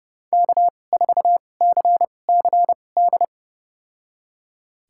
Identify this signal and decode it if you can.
Morse code, heard as a keyed tone, about 30 wpm, spaced as for 22 wpm, 690 Hz. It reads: K4CCD
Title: USA Call Signs - 5 Characters - 2x Character Spacing - 30wpm